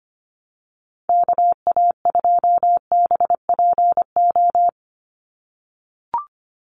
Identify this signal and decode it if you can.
Morse code, heard as a keyed tone, about 25 wpm, 700 Hz. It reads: KA2BPO
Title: USA Call Signs 25wpm